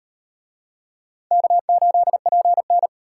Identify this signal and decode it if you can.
Morse code, heard as a keyed tone, about 34 wpm, 700 Hz. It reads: K8PN